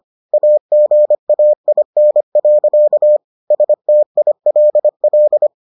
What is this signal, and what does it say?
Morse code, heard as a keyed tone, about 25 wpm, 600 Hz. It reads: AGAIN. STILL